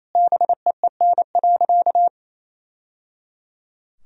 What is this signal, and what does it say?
Morse code, heard as a keyed tone, about 27 wpm, 700 Hz. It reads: BEEN.
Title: Sentences from Top 1000 Words - Review Part 4 - 28wpm